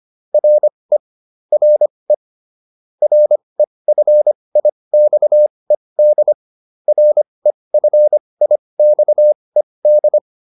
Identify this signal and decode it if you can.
Morse code, heard as a keyed tone, about 25 wpm, spaced as for 20 wpm, 600 Hz. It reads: RE RE REFIXED REFIXED
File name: RE7 20